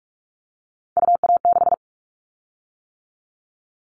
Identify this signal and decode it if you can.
Morse code, heard as a keyed tone, about 46 wpm, 700 Hz. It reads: UA6
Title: Callsign Prefixes for Top 25 DXCC - 45wpm